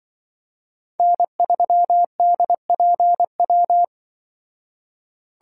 Morse code, heard as a keyed tone, about 24 wpm, 700 Hz. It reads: N3DPW
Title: Speed Racing - USA 1x3 Call Signs 20wpm